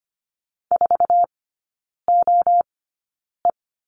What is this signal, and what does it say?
Morse code, heard as a keyed tone, about 25 wpm, spaced as for 9 wpm, 700 Hz. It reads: 4OE
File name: course / 25:10wpm / 093 Lesson 16e - Letter c - Three Characters 25wpm